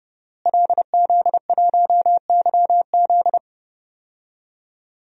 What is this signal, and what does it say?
Morse code, heard as a keyed tone, about 30 wpm, 700 Hz. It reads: LZ1YZ
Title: DXCC-100 Call Signs - 5 Characters - Repeated 2x - 30wpm